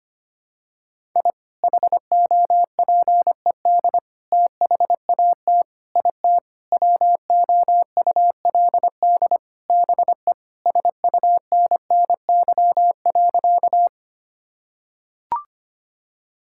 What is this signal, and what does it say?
Morse code, heard as a keyed tone, about 25 wpm, 700 Hz. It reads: I HOPED THAT IT WOULD BE SUNNY.
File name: practice / 25wpm / Sentences with 4-Word Phrases - 25wpm